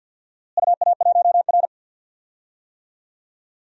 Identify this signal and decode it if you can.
Morse code, heard as a keyed tone, about 53 wpm, 700 Hz. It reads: UA1R
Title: DXCC-100 Call Signs - 4 Characters - Repeated 2x - 50wpm